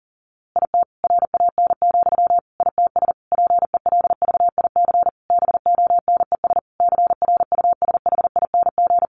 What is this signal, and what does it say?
Morse code, heard as a keyed tone, about 40 wpm, 700 Hz. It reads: IT RAN, ITS PELVIC BONES CRUSHING
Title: A Sound Of Thunder 40wpm